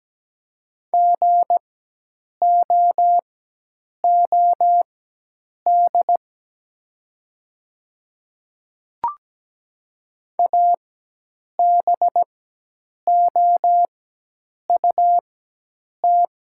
Morse code, heard as a keyed tone, about 17 wpm, spaced as for 8 wpm, 700 Hz. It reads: GOOD ABOUT
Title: 4x Character Spacing - Top 200 Words - 17wpm